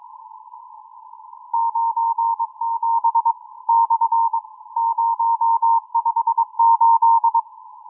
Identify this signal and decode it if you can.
Morse code, heard as a keyed tone, about 22 wpm, 920 Hz. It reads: 97 / 058